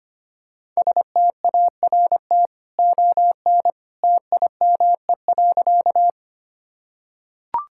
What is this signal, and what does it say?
Morse code, heard as a keyed tone, about 25 wpm, 700 Hz. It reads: START ON TIME.
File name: Easy Sentences with Hints - 25wpm